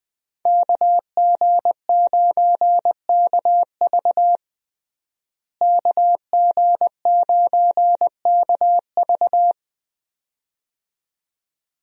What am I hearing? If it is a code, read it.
Morse code, heard as a keyed tone, about 20 wpm, 700 Hz. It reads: KG9KV KG9KV
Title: USA Call Signs - 5 Characters - Repeated 2x - 20wpm